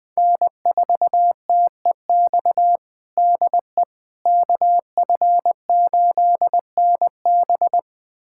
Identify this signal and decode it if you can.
Morse code, heard as a keyed tone, about 20 wpm, 700 Hz. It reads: N4TEX DE KF8NB
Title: USA Callsign DE Callsign - 5 Characters - 20wpm